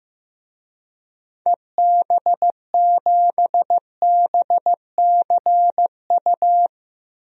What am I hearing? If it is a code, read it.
Morse code, heard as a keyed tone, about 15 wpm, 700 Hz. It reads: EB7BCU
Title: DXCC-100 Call Signs - 6 Characters - Repeated 3x - 15wpm